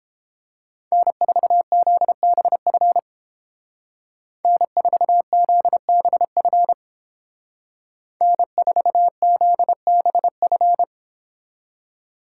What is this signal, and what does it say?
Morse code, heard as a keyed tone, about 29 wpm, 700 Hz. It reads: N4ZBF N4ZBF N4ZBF